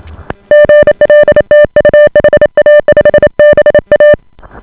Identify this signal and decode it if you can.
Morse code, heard as a keyed tone, about 28 wpm, 590 Hz. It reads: GL TU HA5BA